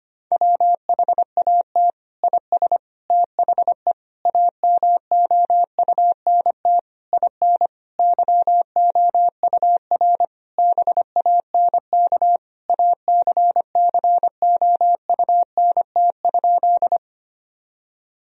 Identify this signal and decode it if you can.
Morse code, heard as a keyed tone, about 25 wpm, 700 Hz. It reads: WHAT IS THE AMOUNT IN YOUR BANK ACCOUNT?
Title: Sentences from Top 1000 Words - Review Part 4 - 25wpm